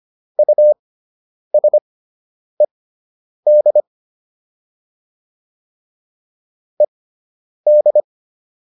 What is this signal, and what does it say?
Morse code, heard as a keyed tone, about 25 wpm, spaced as for 9 wpm, 600 Hz. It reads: USED ED